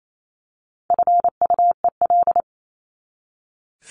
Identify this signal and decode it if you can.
Morse code, heard as a keyed tone, about 28 wpm, 700 Hz. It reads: FUEL
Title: Top 1751-2000 Words 28wpm